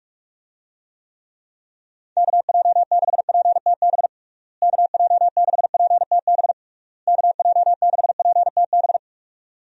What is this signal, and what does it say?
Morse code, heard as a keyed tone, about 46 wpm, 700 Hz. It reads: KJ6PTB KJ6PTB KJ6PTB
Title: USA Call Signs - 6 Characters - Repeated 3x - 45wpm